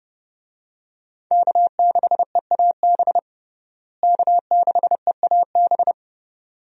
Morse code, heard as a keyed tone, about 30 wpm, 700 Hz. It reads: K6EAB K6EAB
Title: USA Call Signs - 5 Characters - Repeated 2x - 30wpm